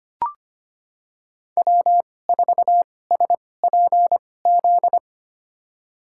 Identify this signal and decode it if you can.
Morse code, heard as a keyed tone, about 25 wpm, spaced as for 18 wpm, 700 Hz. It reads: W4SPZ
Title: USA Call Signs - 5 Characters - 2x Character Spacing - 25wpm